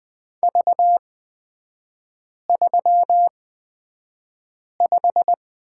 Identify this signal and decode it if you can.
Morse code, heard as a keyed tone, about 20 wpm, spaced as for 5 wpm, 700 Hz. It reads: V35
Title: 250 Lesson 40e - BK - Three Characters 20wpm